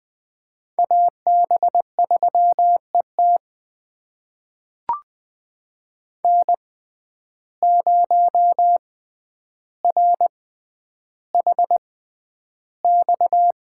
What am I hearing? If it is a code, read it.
Morse code, heard as a keyed tone, about 20 wpm, spaced as for 10 wpm, 700 Hz. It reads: AB3ET N0RHX
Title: USA Call Signs - 5 Characters - 6x Character Spacing - 20wpm